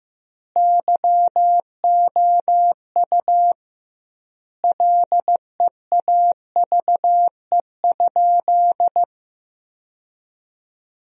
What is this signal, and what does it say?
Morse code, heard as a keyed tone, about 15 wpm, 700 Hz. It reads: YOU LEAVE?
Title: 2x Word-Spacing - Sentences from Top 400 Words - 15wpm